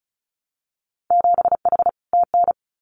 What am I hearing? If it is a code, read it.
Morse code, heard as a keyed tone, about 34 wpm, 700 Hz. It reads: 7H TN
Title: Winter Field Day Exchanges Encore 35wpm